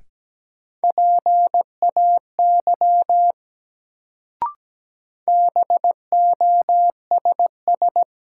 Morse code, heard as a keyed tone, about 17 wpm, 700 Hz. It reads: PAY BOSS
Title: Sentences from Top 700 Words - New Word Drill - 17wpm